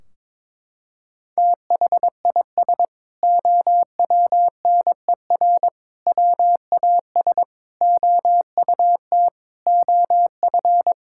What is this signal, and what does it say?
Morse code, heard as a keyed tone, about 22 wpm, 700 Hz. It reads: THIS OWNER WAS OUT OF
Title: Sentences from Top 1600 Words 22wpm